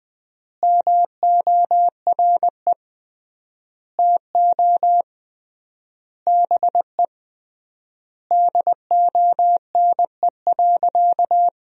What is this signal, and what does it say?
Morse code, heard as a keyed tone, about 21 wpm, 700 Hz. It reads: MORE TO BE DONE.
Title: 3x Word-Spacing - Sentences from Top 500 Words - 20wpm